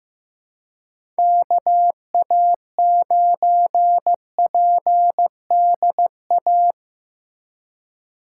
Morse code, heard as a keyed tone, about 15 wpm, 700 Hz. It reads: KA9PDA
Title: USA Call Signs - 2x Character Spacing - 15wpm